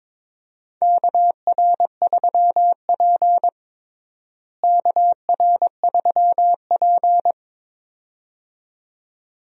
Morse code, heard as a keyed tone, about 22 wpm, 700 Hz. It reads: KR3P KR3P
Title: USA Call Signs - 4 Characters - Repeated 2x - 22wpm